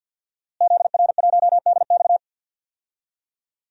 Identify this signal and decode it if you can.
Morse code, heard as a keyed tone, about 47 wpm, 700 Hz. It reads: ZR1DX
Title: DXCC-100 Call Signs - 5 Characters - Repeated 3x - 50wpm